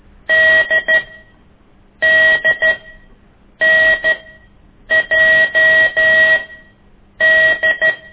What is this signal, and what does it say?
Morse code, heard as a keyed tone, about 12 wpm, spaced as for 7 wpm, 1900 Hz. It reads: DDNJD
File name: ik1weg2 / CORSI CW